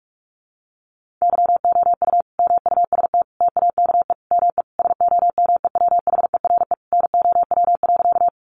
Morse code, heard as a keyed tone, about 44 wpm, 700 Hz. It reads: YOU MUST TAKE ME SOMEWHERE NOW.
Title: Sentences from Top 1400 Words 45wpm